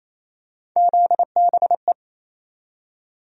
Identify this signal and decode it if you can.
Morse code, heard as a keyed tone, about 27 wpm, 700 Hz. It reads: ZBE
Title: USA Call Sign Postfixes - 3 Letters 28wpm